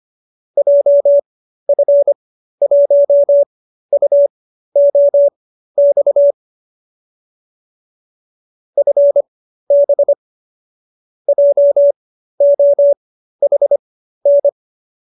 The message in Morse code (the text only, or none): JF1UOX FB JOHN